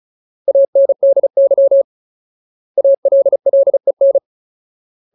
Morse code, heard as a keyed tone, about 34 wpm, 550 Hz. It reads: ANDY ALLEN
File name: Ultimate Word List / 35WPM Ultimate